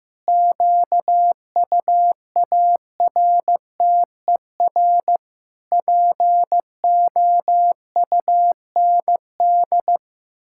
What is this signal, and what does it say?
Morse code, heard as a keyed tone, about 16 wpm, 700 Hz. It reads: QUARTER POUND
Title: Pangram Challenge v2 - 15wpm